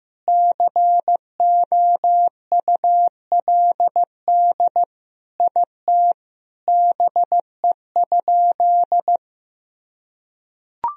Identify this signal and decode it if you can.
Morse code, heard as a keyed tone, about 15 wpm, 700 Hz. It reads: COULD IT BE?
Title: Sentences from Top 1000 Words - Review Part 2 - 15wpm